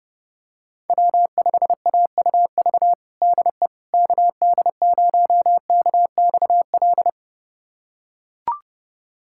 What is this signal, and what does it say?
Morse code, heard as a keyed tone, about 32 wpm, 700 Hz. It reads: W5AUV DE KD0KXL
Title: USA Callsign DE Callsign 30wpm